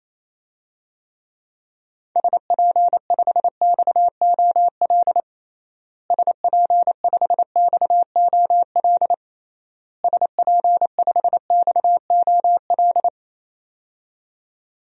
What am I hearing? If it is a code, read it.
Morse code, heard as a keyed tone, about 28 wpm, 700 Hz. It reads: SP5XOL SP5XOL SP5XOL
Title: DXCC-100 Call Signs - 6 Characters - Repeated 3x - 28wpm